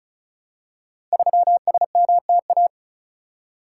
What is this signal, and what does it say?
Morse code, heard as a keyed tone, about 36 wpm, 700 Hz. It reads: 3SMTA
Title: Code Groups - 5 Letter:Numbers - 35wpm